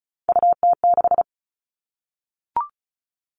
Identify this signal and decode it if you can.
Morse code, heard as a keyed tone, about 34 wpm, 700 Hz. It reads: UT6
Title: All Cut Numbers - 3-Digit 35wpm